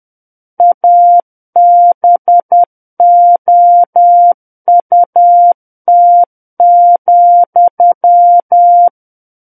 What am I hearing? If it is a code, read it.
Morse code, heard as a keyed tone, about 10 wpm, 700 Hz. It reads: ABOUT,
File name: Book One / 01-Chapter 1 - 10 WPM